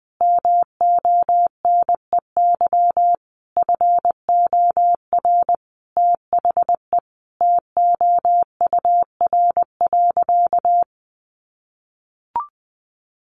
Morse code, heard as a keyed tone, about 20 wpm, 700 Hz. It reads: MONEY FOR THE TOUR.
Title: Sentences from Top 1400 Words 20wpm